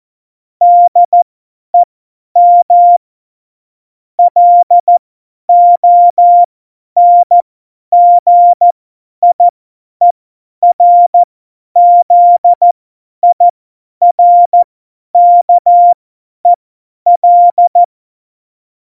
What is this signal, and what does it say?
Morse code, heard as a keyed tone, about 14 wpm, spaced as for 10 wpm, 700 Hz. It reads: DEM LONGIERZIRKEL
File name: Morsekurs / Witze / Witz 04 August 2020, 50 F70